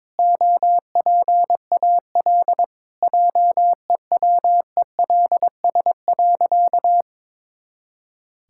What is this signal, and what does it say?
Morse code, heard as a keyed tone, about 22 wpm, 700 Hz. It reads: OPAL JEWELS.
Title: Pangram Challenge v2 - 22wpm